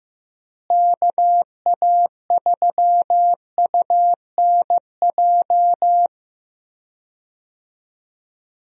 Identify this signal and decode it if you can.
Morse code, heard as a keyed tone, about 16 wpm, 700 Hz. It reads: KA3UNJ
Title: USA Call Signs - 6 Characters - Repeated 3x - 15wpm